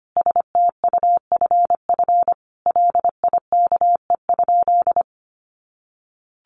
Morse code, heard as a keyed tone, about 25 wpm, 700 Hz. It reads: STUFF LIKE?
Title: Sentences from Top 1400 Words 25wpm